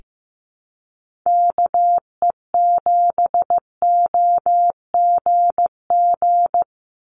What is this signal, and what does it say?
Morse code, heard as a keyed tone, about 15 wpm, 700 Hz. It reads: KE7OGG
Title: USA Call Signs 15wpm